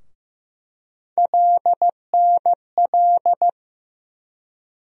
Morse code, heard as a keyed tone, about 15 wpm, 700 Hz. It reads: LNL
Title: USA Call Sign Postfixes - 3 Letters 15wpm